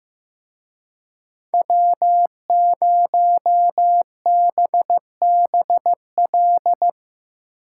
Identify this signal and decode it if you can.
Morse code, heard as a keyed tone, about 17 wpm, 700 Hz. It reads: W0BBL
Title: Speed Racing - USA 1x3 Call Signs 15wpm